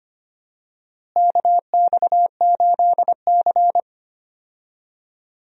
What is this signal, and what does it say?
Morse code, heard as a keyed tone, about 25 wpm, 700 Hz. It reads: KX8C
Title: USA Call Signs - 4 Characters - Repeated 2x - 25wpm